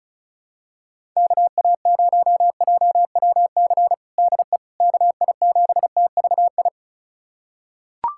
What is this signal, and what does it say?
Morse code, heard as a keyed tone, about 36 wpm, 700 Hz. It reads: KA0JWC DE KI7TVI
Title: USA Callsign DE Callsign - 6 Characters - 35wpm